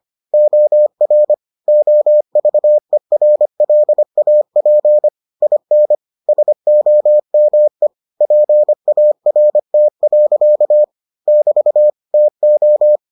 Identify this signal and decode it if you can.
Morse code, heard as a keyed tone, about 25 wpm, 600 Hz. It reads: OR OVERLAP IN SOME PART. <BT> TO